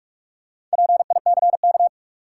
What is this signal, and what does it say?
Morse code, heard as a keyed tone, about 46 wpm, 700 Hz. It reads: PICK
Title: Sentences from Top 1000 Words - New Word Drill - 45wpm